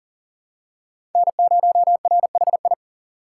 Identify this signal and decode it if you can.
Morse code, heard as a keyed tone, about 40 wpm, 700 Hz. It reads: N0RHI
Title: USA Call Signs - 5 Characters - 4x Character Spacing - 40wpm